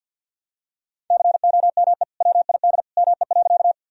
Morse code, heard as a keyed tone, about 50 wpm, 700 Hz. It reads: COME WID ME.
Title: Sentences from Top 200 Words and Abbreviations 50wpm